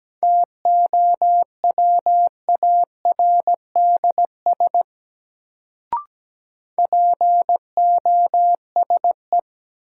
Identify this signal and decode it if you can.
Morse code, heard as a keyed tone, about 17 wpm, 700 Hz. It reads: TOWARDS POSE
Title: Sentences from Top 1000 Words - Review Part 2 - New Word Drill - 17wpm